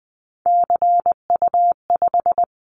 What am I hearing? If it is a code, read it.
Morse code, heard as a keyed tone, about 20 wpm, 700 Hz. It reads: CU5
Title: Callsign Prefixes for Top 50 DXCC - 20wpm